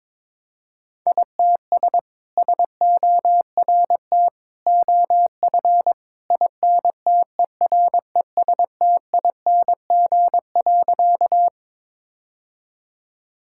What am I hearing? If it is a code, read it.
Morse code, heard as a keyed tone, about 21 wpm, 700 Hz. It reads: ITS SORT OF INTERESTING.
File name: practice / 22wpm / Sentences with 2-Word Phrases - 22wpm